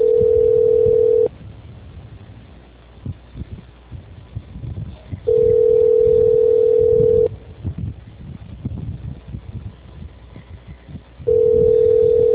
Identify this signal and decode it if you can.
Ringback tone (North America), 440 + 480 Hz.
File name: out- -2036- - - .53